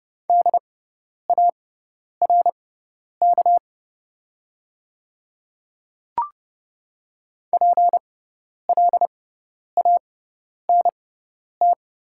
Morse code, heard as a keyed tone, about 31 wpm, spaced as for 10 wpm, 690 Hz. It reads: DARK PLANT
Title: 6x Character Spacing - Top 500 Words - 30wpm